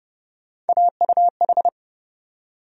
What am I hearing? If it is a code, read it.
Morse code, heard as a keyed tone, about 31 wpm, 700 Hz. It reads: AUH
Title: USA Call Sign Postfixes - 3 Letters 30wpm